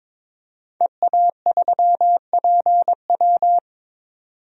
Morse code, heard as a keyed tone, about 22 wpm, 700 Hz. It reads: EA3PW
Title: DXCC-100 Call Signs - 5 Characters - 22wpm